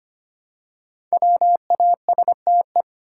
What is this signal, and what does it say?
Morse code, heard as a keyed tone, about 25 wpm, 700 Hz. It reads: WASTE
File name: 5 Letter Words - First 3 Letters Given - 25wpm